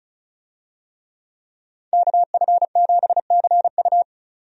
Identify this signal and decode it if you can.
Morse code, heard as a keyed tone, about 34 wpm, 700 Hz. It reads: KF7CU
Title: USA Call Signs - 5 Characters - 35wpm